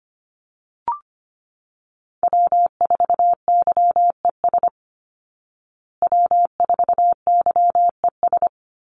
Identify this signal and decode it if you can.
Morse code, heard as a keyed tone, about 25 wpm, 700 Hz. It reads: W4YES W4YES
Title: Call Signs Repeated 3x - 25wpm